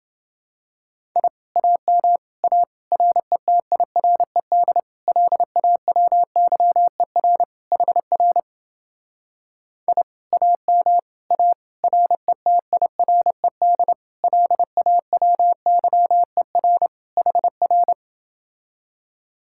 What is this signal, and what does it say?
Morse code, heard as a keyed tone, about 28 wpm, 700 Hz. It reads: I AM A RETIRED LAWYER HR I AM A RETIRED LAWYER HR